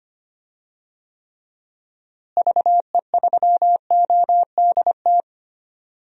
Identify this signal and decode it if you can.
Morse code, heard as a keyed tone, about 25 wpm, 700 Hz. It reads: VE3ODT
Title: DXCC-100 Call Signs - 6 Characters - Repeated 3x - 25wpm